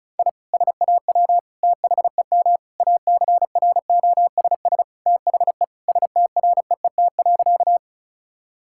Morse code, heard as a keyed tone, about 36 wpm, 700 Hz. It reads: I SAW THEM ACROSS THE STREET.
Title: Sentences from Top 1000 Words - Review Part 5 - 35wpm